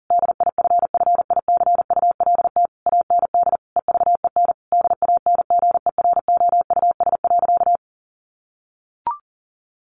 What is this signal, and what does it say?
Morse code, heard as a keyed tone, about 40 wpm, 700 Hz. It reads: DIFFICULT AND EVEN DANGEROUS.
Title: Sentences from Top 1200 Words 40wpm